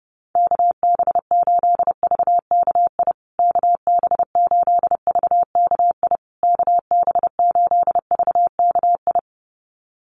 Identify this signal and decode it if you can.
Morse code, heard as a keyed tone, about 30 wpm, 700 Hz. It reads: KB8VKI KB8VKI KB8VKI
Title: USA Call Signs Repeated 3x 30wpm